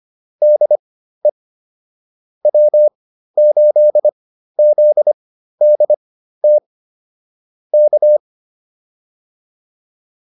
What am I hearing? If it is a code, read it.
Morse code, heard as a keyed tone, about 25 wpm, spaced as for 13 wpm, 600 Hz. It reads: DE W8ZDT K
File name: QSO 108 13